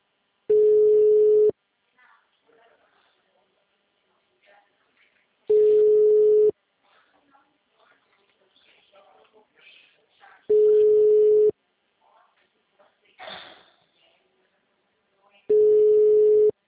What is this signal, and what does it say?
Ringback tone (Europe, 425 Hz), with speech.
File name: out- -2001- - - .4268